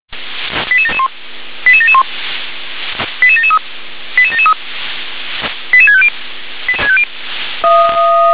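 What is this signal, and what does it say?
Five-tone selective call (ZVEI), heard as quick runs of five tones, tones of about 70 ms. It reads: 88771 88771 88773 88773 88650
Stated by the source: ZVEI five-tone selective call 88702 received on SDR.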